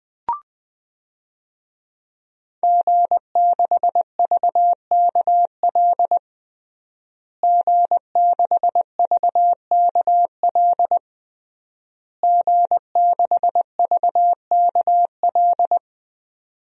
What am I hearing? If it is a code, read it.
Morse code, heard as a keyed tone, about 20 wpm, 700 Hz. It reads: G6VKL G6VKL G6VKL
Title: DXCC-100 Call Signs - 5 Characters - Repeated 3x - 20wpm